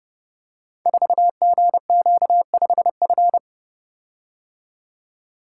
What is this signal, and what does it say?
Morse code, heard as a keyed tone, about 30 wpm, 700 Hz. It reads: 4GQ5F
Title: Code Groups - 5 Letter:Numbers - 30wpm